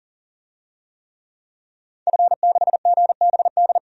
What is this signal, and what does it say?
Morse code, heard as a keyed tone, about 38 wpm, 700 Hz. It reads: F6GBD
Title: DXCC-100 Call Signs - 5 Characters - 40wpm